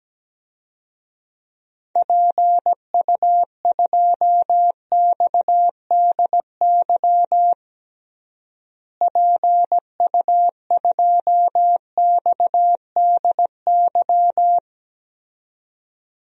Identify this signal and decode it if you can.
Morse code, heard as a keyed tone, about 17 wpm, 700 Hz. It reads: PU2XDY PU2XDY
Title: DXCC-100 Call Signs - 6 Characters - Repeated 2x - 17wpm